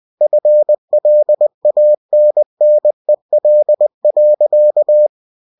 Morse code, heard as a keyed tone, about 20 wpm, 600 Hz. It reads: FLANNEL.